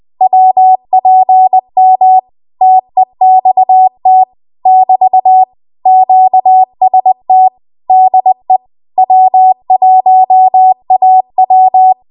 Morse code, heard as a keyed tone, about 20 wpm, 750 Hz. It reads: WPM TEXT <BT> QST DE W1AW
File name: Archive / 20 WPM / 20WPM